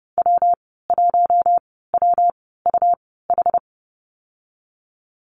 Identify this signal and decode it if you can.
Morse code, heard as a keyed tone, about 30 wpm, 700 Hz. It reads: W 1 W U H
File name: Lesson 19g - Letter w - Call Signs 30wpm